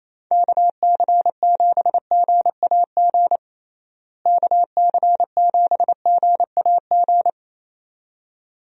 Morse code, heard as a keyed tone, about 28 wpm, 700 Hz. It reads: KC7GAG KC7GAG
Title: USA Call Signs - 6 Characters - Repeated 2x - 28wpm